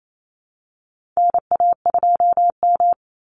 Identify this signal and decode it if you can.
Morse code, heard as a keyed tone, about 28 wpm, 700 Hz. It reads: NA2M
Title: Call Signs - 28wpm